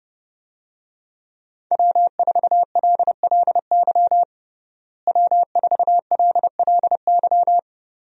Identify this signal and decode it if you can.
Morse code, heard as a keyed tone, about 30 wpm, 700 Hz. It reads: W4LLY W4LLY
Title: USA Call Signs - 5 Characters - Repeated 2x - 30wpm